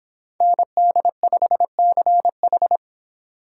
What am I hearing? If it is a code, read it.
Morse code, heard as a keyed tone, about 25 wpm, 700 Hz. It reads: ND5CH